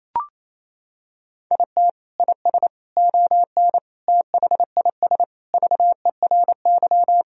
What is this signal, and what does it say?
Morse code, heard as a keyed tone, about 28 wpm, 700 Hz. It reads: IT IS ON THIS VERY
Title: Sentences from Top 1000 Words - Review Part 1 - 28wpm